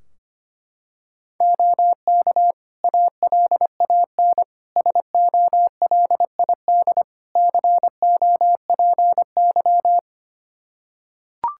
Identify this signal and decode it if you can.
Morse code, heard as a keyed tone, about 25 wpm, 700 Hz. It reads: OK ALAN SOLID COPY
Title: Practice QSO - Exchange 3 - More Info - 25wpm